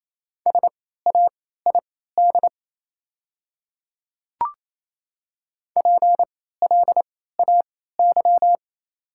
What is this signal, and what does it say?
Morse code, heard as a keyed tone, about 28 wpm, spaced as for 16 wpm, 700 Hz. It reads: SAID PLAY
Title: 3x Character Spacing - Top 200 Words - 28wpm